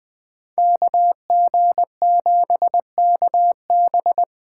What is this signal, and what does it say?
Morse code, heard as a keyed tone, about 21 wpm, 700 Hz. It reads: KG7KB